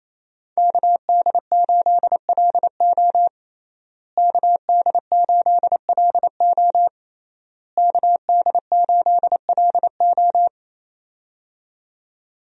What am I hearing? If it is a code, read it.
Morse code, heard as a keyed tone, about 28 wpm, 700 Hz. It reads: KD8LO KD8LO KD8LO
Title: USA Call Signs - 5 Characters - Repeated 3x - 28wpm